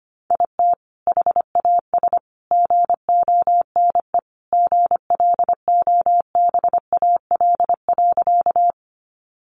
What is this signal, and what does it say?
Morse code, heard as a keyed tone, about 25 wpm, 700 Hz. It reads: IT HAS GONE GLOBAL.
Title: Sentences from Top 1200 Words 25wpm